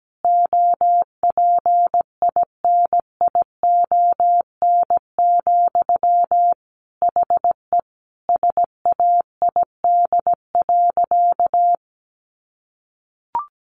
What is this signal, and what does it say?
Morse code, heard as a keyed tone, about 17 wpm, 700 Hz. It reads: OPINION, HE SAID.
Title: Sentences from Top 1200 Words 17wpm